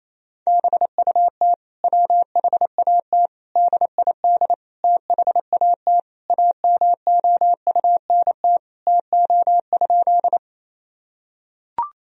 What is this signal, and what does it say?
Morse code, heard as a keyed tone, about 27 wpm, 700 Hz. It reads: BUT WHAT DID THAT AMOUNT TO?
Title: Sentences from Top 1000 Words - Review Part 4 - 28wpm